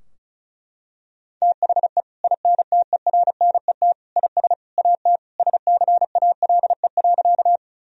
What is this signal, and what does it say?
Morse code, heard as a keyed tone, about 36 wpm, 700 Hz. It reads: THE INTERNET IS AT SCALE.
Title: Sentences from Top 1000 Words - Review Part 4 - 35wpm